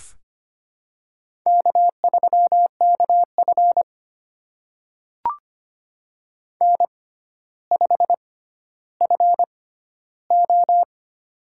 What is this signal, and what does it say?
Morse code, heard as a keyed tone, about 25 wpm, spaced as for 13 wpm, 700 Hz. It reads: K3KF N5FO
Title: USA Call Signs - 6x Character Spacing - 25wpm